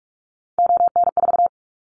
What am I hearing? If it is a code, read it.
Morse code, heard as a keyed tone, about 44 wpm, 700 Hz. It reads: ON4
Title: Callsign Prefixes for Top 100 DXCC - 45wpm